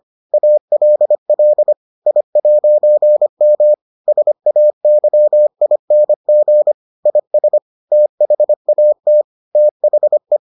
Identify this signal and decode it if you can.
Morse code, heard as a keyed tone, about 25 wpm, 600 Hz. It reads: ALL I'M SAYING IS THAT THE